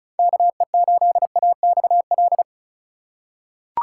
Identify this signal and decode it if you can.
Morse code, heard as a keyed tone, about 36 wpm, 700 Hz. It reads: KE8AXL
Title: USA Call Signs - 6 Characters - 4x Character Spacing - 35wpm